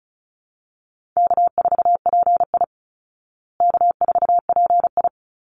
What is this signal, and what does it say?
Morse code, heard as a keyed tone, about 34 wpm, 700 Hz. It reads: K4PI K4PI
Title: Easy Call Signs Repeated 2x - 35wpm